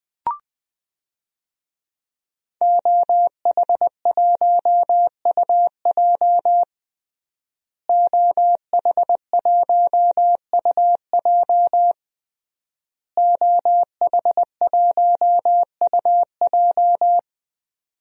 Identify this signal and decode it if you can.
Morse code, heard as a keyed tone, about 20 wpm, 700 Hz. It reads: OH1UJ OH1UJ OH1UJ
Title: DXCC-100 Call Signs - 5 Characters - Repeated 3x - 20wpm